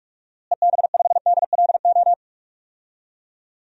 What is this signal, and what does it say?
Morse code, heard as a keyed tone, about 41 wpm, 700 Hz. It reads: EB5DLO
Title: DXCC-100 Call Signs - 6 Characters - Repeated 2x - 45wpm